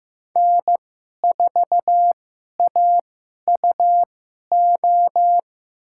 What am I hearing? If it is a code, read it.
Morse code, heard as a keyed tone, about 15 wpm, spaced as for 11 wpm, 700 Hz. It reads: N4AUO
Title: USA Call Signs - 5 Characters - 2x Character Spacing - 15wpm